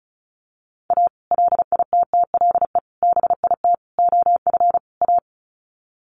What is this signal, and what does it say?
Morse code, heard as a keyed tone, about 34 wpm, 700 Hz. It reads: A LITTLE BIT OF A
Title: Sets of 5 Words 35wpm